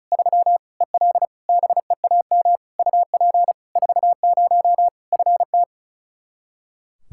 Morse code, heard as a keyed tone, about 34 wpm, 700 Hz. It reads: 3 EL BEAM UP 40 FT
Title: Practice QSO - Exchange 2 - Info - 35wpm